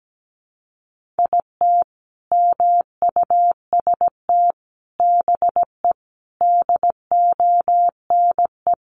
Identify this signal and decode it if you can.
Morse code, heard as a keyed tone, about 17 wpm, 700 Hz. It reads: IT MUST BE DONE
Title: Sentences from Top 1100 Words 17wpm